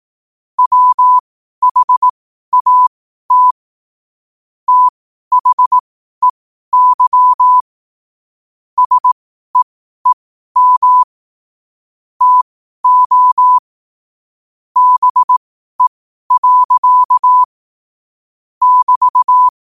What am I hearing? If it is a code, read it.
Morse code, heard as a keyed tone, about 18 wpm, spaced as for 12 wpm, 1000 Hz. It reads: WHAT THEY SEEM TO BE. <BT>
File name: qotdfiles / 12wpm